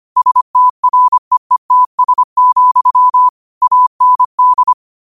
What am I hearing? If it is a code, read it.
Morse code, heard as a keyed tone, about 25 wpm, 1000 Hz. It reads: STREETS, AND
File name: qotdfiles / 25wpm